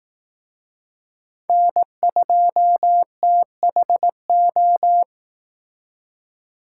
Morse code, heard as a keyed tone, about 18 wpm, 700 Hz. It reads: N2THO
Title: Speed Racing - USA Call Signs 15wpm